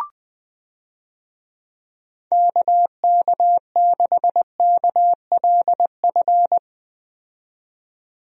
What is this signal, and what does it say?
Morse code, heard as a keyed tone, about 20 wpm, 700 Hz. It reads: KK6KLF
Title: USA Call Signs - 6 Characters - 20wpm